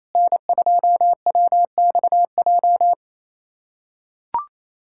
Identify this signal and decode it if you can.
Morse code, heard as a keyed tone, about 28 wpm, 690 Hz. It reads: N2WXJ
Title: USA Callsign DE Callsign - 5 Characters - 28wpm